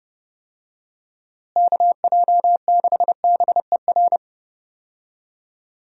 Morse code, heard as a keyed tone, about 30 wpm, 700 Hz. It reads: KJ6BER
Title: USA Call Signs - 6 Characters - Repeated 3x - 30wpm